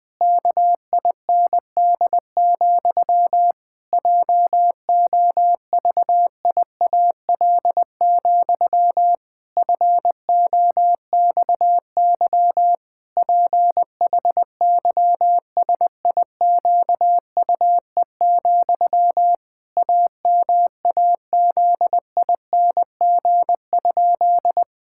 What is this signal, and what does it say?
Morse code, heard as a keyed tone, about 20 wpm, 700 Hz. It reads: KIND, JOVIAL, FOXY PHYSIQUE, AMAZING?
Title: Pangram Challenge v2 - 20wpm